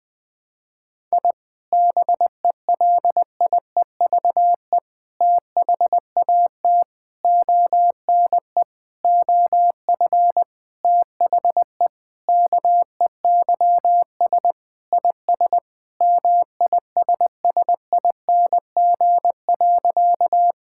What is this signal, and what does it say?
Morse code, heard as a keyed tone, about 20 wpm, 700 Hz. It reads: I BELIEVE THAT ONE OF THE KEYS IS MISSING.
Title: Sentences with 4-Word Phrases - 20wpm